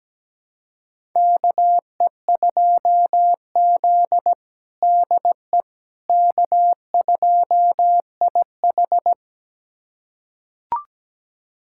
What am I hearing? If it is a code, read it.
Morse code, heard as a keyed tone, about 17 wpm, 700 Hz. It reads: KE2Z DE K2IH
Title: USA Callsign DE Callsign - 4 Characters - 17wpm